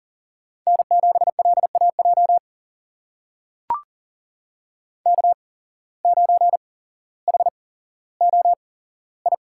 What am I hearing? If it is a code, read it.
Morse code, heard as a keyed tone, about 40 wpm, spaced as for 17 wpm, 700 Hz. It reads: N7LAJ K9HOI
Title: USA Call Signs - 5 Characters - 8x Character Spacing - 40wpm